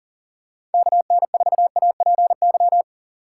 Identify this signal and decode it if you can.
Morse code, heard as a keyed tone, about 40 wpm, 700 Hz. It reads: KN4APY
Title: USA Call Signs - 6 Characters - 2x Character Spacing - 40wpm